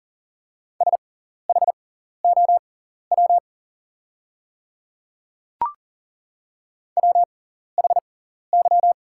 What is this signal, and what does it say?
Morse code, heard as a keyed tone, about 40 wpm, spaced as for 14 wpm, 700 Hz. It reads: SHOW WHY
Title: 6x Character Spacing - Top 300 Words - 40wpm